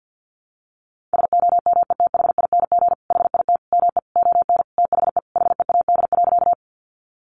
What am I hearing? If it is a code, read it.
Morse code, heard as a keyed tone, about 50 wpm, 700 Hz. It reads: SOMETHING HIT ME ON THE HEAD.
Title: Sentences from Top 1200 Words 50wpm